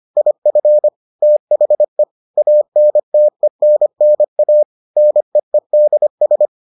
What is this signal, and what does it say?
Morse code, heard as a keyed tone, about 25 wpm, 600 Hz. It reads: IF THE ANTENNA NEEDS